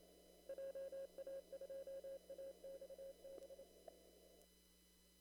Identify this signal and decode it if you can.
Morse code, heard as a keyed tone, about 26 wpm, 540 Hz. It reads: JA2AXB
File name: CW / kt5j wpx cw-op-k5pi